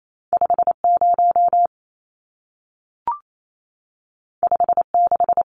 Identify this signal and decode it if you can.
Morse code, heard as a keyed tone, about 28 wpm, 700 Hz. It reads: 50 56
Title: Numbers - 2-Digit 28wpm